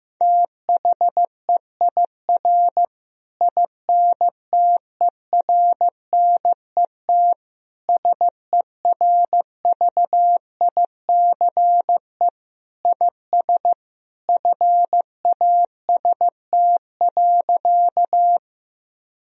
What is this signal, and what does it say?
Morse code, heard as a keyed tone, about 15 wpm, 700 Hz. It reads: THEIR INTERNET SERVICE IS FAST.
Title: Sentences from Top 1000 Words - Review Part 4 - 15wpm